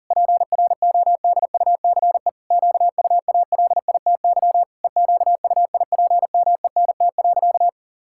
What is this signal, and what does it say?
Morse code, heard as a keyed tone, about 40 wpm, 700 Hz. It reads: PRODUCE QUALITY EQUIPMENT.
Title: Sentences from Top 1000 Words - Review Part 6 - 40wpm